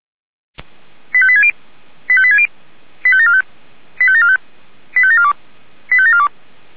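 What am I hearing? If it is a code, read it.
Five-tone selective call (ZVEI), heard as quick runs of five tones, tones of about 70 ms. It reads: 85860 85860 85745 85745 85742 85742
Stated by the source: Five-tone ZVEI selective call received by SDR, from the Orlamünde fire brigade (feuerwehr-orlamuende) audio archive.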